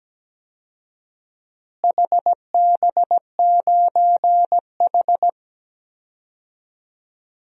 Morse code, heard as a keyed tone, about 17 wpm, 700 Hz. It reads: HB9H
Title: DXCC-100 Call Signs - 4 Characters - Repeated 3x - 17wpm